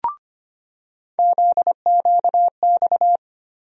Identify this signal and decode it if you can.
Morse code, heard as a keyed tone, about 25 wpm, 700 Hz. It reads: ZQX
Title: USA Call Sign Postfixes - 3 Letters 25wpm